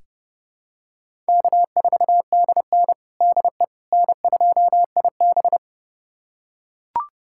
Morse code, heard as a keyed tone, about 32 wpm, 700 Hz. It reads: K4DN DE N2IB
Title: USA Callsign DE Callsign - 4 Characters - 30wpm